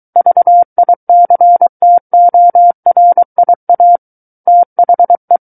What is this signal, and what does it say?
Morse code, heard as a keyed tone, about 23 wpm, 700 Hz. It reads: VICTORIA THE
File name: Book One / 14-Chapter 14 - 23 WPM